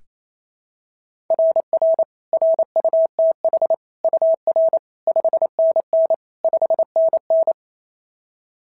Morse code, heard as a keyed tone, about 28 wpm, 650 Hz. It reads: RR RUTH UR 5NN 5NN
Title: SOTA QSO Practice 28wpm